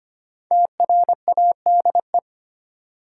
Morse code, heard as a keyed tone, about 25 wpm, 700 Hz. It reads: TRADE
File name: 5 Letter Words - First 3 Letters Given - 25wpm